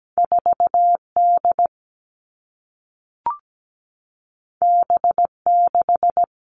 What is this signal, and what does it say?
Morse code, heard as a keyed tone, about 17 wpm, 700 Hz. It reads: 4D B6
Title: All Cut Numbers - 2-Digit 17wpm